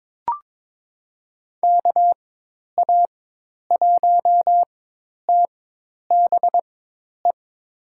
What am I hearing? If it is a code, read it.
Morse code, heard as a keyed tone, about 21 wpm, spaced as for 10 wpm, 700 Hz. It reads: KA1TBE
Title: USA Call Signs - 6 Characters - 4x Character Spacing - 22wpm